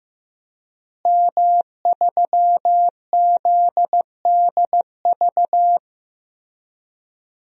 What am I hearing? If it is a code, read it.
Morse code, heard as a keyed tone, about 15 wpm, 700 Hz. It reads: M3ZDV
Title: DXCC-100 Call Signs - 5 Characters - Repeated 3x - 15wpm